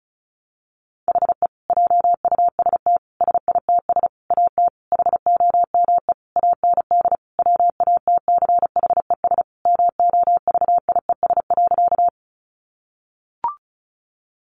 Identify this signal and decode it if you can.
Morse code, heard as a keyed tone, about 34 wpm, 700 Hz. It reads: HE JUST SITS AT HOME AND WATCHES MOVIES.
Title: Additional Easy Phrases 35wpm